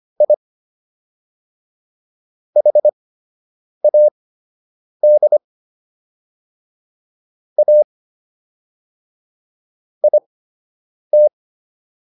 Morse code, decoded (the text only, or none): I HAD A IT